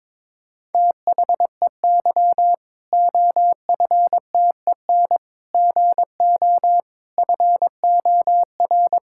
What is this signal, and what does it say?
Morse code, heard as a keyed tone, about 22 wpm, 700 Hz. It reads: THEY OFTEN GO FOR
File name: Difficult Sentences with Hints - 22wpm